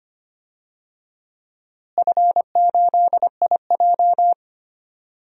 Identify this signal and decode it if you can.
Morse code, heard as a keyed tone, about 25 wpm, 700 Hz. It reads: F8IJ
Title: DXCC-100 Call Signs - 4 Characters - Repeated 3x - 25wpm